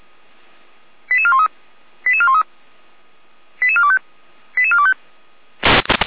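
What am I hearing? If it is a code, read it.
Five-tone selective call (ZVEI), heard as quick runs of five tones, tones of about 70 ms. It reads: 88413 88413 88425 88425
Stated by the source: ZVEI five-tone selective call received via SDR.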